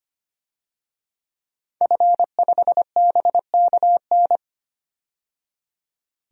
Morse code, heard as a keyed tone, about 25 wpm, 700 Hz. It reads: F5BKN
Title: DXCC-100 Call Signs - 5 Characters - 25wpm